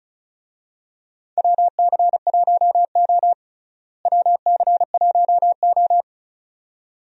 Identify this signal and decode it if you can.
Morse code, heard as a keyed tone, about 36 wpm, 700 Hz. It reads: WC1O WC1O
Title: USA Call Signs - 4 Characters - Repeated 2x - 35wpm